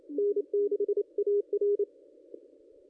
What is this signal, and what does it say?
Morse code, heard as a keyed tone, about 28 wpm, 410 Hz. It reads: N6AR
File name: contests / ss-cw-k5tr-op-n5rz -1800